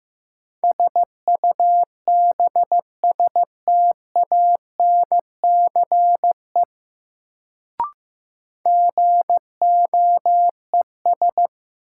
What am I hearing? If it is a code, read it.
Morse code, heard as a keyed tone, about 15 wpm, 700 Hz. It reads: SUBSTANCE GOES
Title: Sentences from Top 1500 Words - New Word Drill Alt - 15wpm